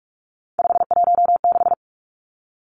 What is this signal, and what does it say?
Morse code, heard as a keyed tone, about 46 wpm, 700 Hz. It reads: 516
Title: Three Challenging Letter-Number Combinations 45wpm